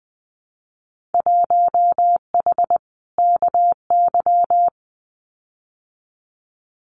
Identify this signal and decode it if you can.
Morse code, heard as a keyed tone, about 20 wpm, 700 Hz. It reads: 1H KY
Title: Winter Field Day Exchanges 20wpm